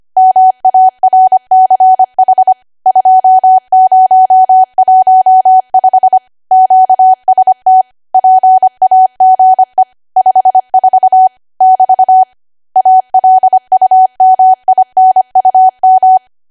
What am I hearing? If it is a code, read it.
Morse code, heard as a keyed tone, about 25 wpm, 750 Hz. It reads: MARCH 2015 QST PAGE 54 <BT> ALUMINUM
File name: Archive / 25 WPM / 25WPM